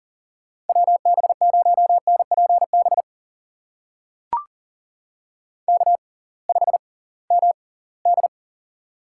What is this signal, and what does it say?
Morse code, heard as a keyed tone, about 38 wpm, spaced as for 23 wpm, 700 Hz. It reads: WB0NPB K5MD
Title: USA Call Signs - 6x Character Spacing - 40wpm